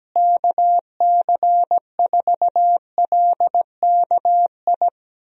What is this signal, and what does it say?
Morse code, heard as a keyed tone, about 17 wpm, 700 Hz. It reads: KC4LKI
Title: USA Call Signs - 6 Characters - Repeated 3x - 17wpm